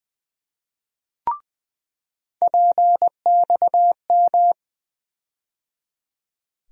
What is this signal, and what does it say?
Morse code, heard as a keyed tone, about 20 wpm, 700 Hz. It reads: PXM
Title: USA Call Sign Postfixes - 3 Letters 20wpm